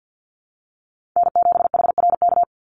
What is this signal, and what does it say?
Morse code, heard as a keyed tone, about 48 wpm, 700 Hz. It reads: N7HRK
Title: Call Signs - 50wpm